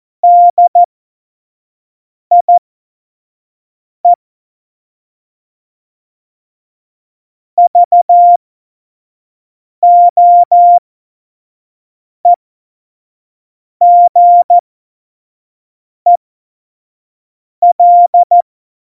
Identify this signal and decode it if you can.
Morse code, heard as a keyed tone, about 14 wpm, spaced as for 5 wpm, 700 Hz. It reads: DIE VOEGEL
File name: Morsekurs / Witze / Witz 22. August 17, Tempo 25 F70